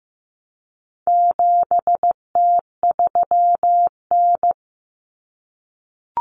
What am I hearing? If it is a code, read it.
Morse code, heard as a keyed tone, about 15 wpm, 700 Hz. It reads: 7T3N
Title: Common Cut Numbers - 4-Digit 15wpm